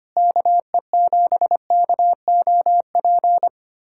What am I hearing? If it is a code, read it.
Morse code, heard as a keyed tone, about 25 wpm, 700 Hz. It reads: KE7KOP
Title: USA Callsign DE Callsign - 6 Characters - 25wpm